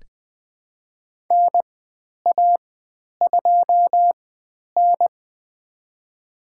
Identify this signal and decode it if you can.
Morse code, heard as a keyed tone, about 20 wpm, spaced as for 10 wpm, 700 Hz. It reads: NA2N
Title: 109 Lesson 18g - Letter m - Call Signs 20wpm